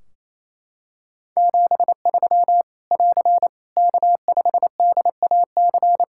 Morse code, heard as a keyed tone, about 28 wpm, 700 Hz. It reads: 73 <AR> K5DAC
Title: Practice QSO - Exchange 4 - Ending - 28wpm